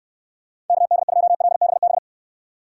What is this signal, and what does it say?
Morse code, heard as a keyed tone, about 69 wpm, 700 Hz. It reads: KD2LBB